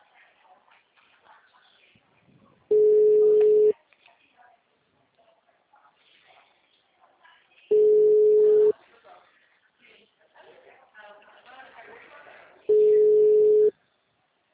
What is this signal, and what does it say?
Ringback tone (Europe, 425 Hz), with speech.